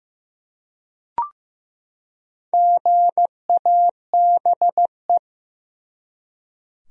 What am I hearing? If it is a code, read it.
Morse code, heard as a keyed tone, about 15 wpm, 700 Hz. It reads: GABE
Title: USA Common Names - 15wpm